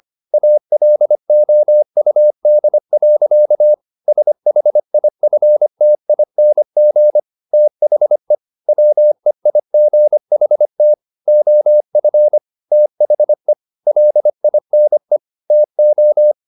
Morse code, heard as a keyed tone, about 25 wpm, 600 Hz. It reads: ALOUD. SHIFTING THE WEIGHT OF THE LINE TO
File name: the old man and the sea part 1 25wpm